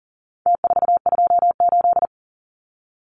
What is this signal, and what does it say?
Morse code, heard as a keyed tone, about 40 wpm, 700 Hz. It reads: T428
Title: Numbers - Contest Style - 4-Digit 40wpm